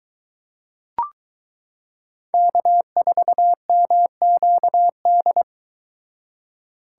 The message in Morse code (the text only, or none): K4MQD